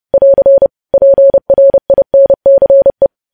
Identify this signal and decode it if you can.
Morse code, heard as a keyed tone, about 30 wpm, 550 Hz. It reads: <AR> PRINCE
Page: news headlines in morse code at 30 wpm